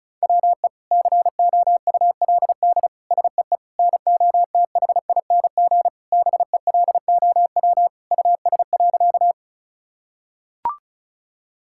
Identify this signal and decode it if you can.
Morse code, heard as a keyed tone, about 34 wpm, 700 Hz. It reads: WE COULD SEE NOTHING BELOW US.
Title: Sentences from Top 1000 Words - Review Part 5 - 35wpm